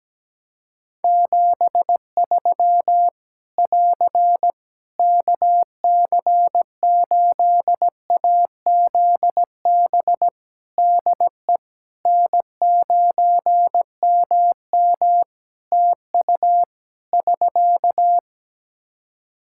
Morse code, heard as a keyed tone, about 17 wpm, 700 Hz. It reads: 73 <AR> KC8AZB DE N9MM TU <SK>
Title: K1USN SST QSO Practice 17wpm